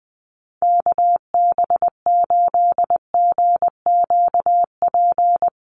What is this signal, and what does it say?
Morse code, heard as a keyed tone, about 20 wpm, 700 Hz. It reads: KB8GQP
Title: USA Call Signs Repeated 3x 20wpm